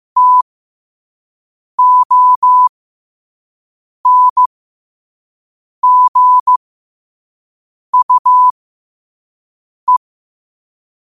Morse code, heard as a keyed tone, about 15 wpm, spaced as for 5 wpm, 1000 Hz. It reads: TONGUE
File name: qotdfiles / 5wpm